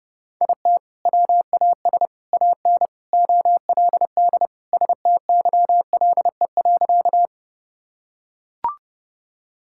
Morse code, heard as a keyed tone, about 32 wpm, 700 Hz. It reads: IT WAS AN OLD STYLE.
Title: Sentences from Top 1000 Words - Review Part 5 - 30wpm